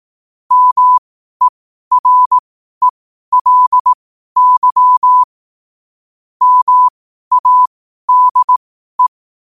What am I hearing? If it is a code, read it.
Morse code, heard as a keyed tone, about 18 wpm, spaced as for 12 wpm, 1000 Hz. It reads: MERELY MADE
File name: qotdfiles / 12wpm